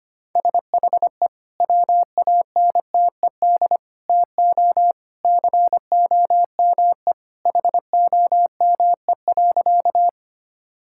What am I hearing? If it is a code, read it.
Morse code, heard as a keyed tone, about 25 wpm, 700 Hz. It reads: SHE WANTED TO COME HOME.
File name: practice / 25wpm / Sentences with 2-Word Phrases - 25wpm